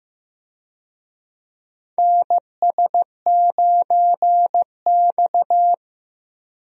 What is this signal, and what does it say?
Morse code, heard as a keyed tone, about 15 wpm, 700 Hz. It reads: NS9X
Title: USA Call Signs - 4 Characters - Repeated 2x - 15wpm